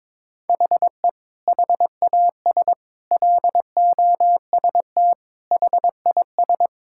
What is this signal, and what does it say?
Morse code, heard as a keyed tone, about 22 wpm, 700 Hz. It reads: HE HAS LOST HIS
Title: Sentences from Top 1600 Words 22wpm